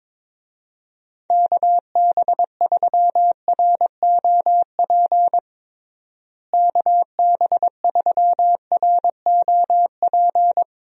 Morse code, heard as a keyed tone, about 22 wpm, 700 Hz. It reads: KB3ROP KB3ROP
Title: USA Call Signs - 6 Characters - Repeated 3x - 22wpm